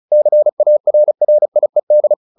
Morse code, heard as a keyed tone, about 34 wpm, 600 Hz. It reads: CARRIED
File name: the old man and the sea 2 35wpm